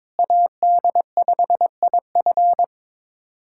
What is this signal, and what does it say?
Morse code, heard as a keyed tone, about 22 wpm, 700 Hz. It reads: AD5IF
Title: USA Callsign DE Callsign - 5 Characters - 22wpm